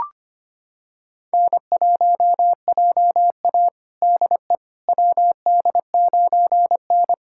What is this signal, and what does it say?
Morse code, heard as a keyed tone, about 25 wpm, 700 Hz. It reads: N1JA DE WD9N
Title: USA Callsign DE Callsign - 4 Characters - 25wpm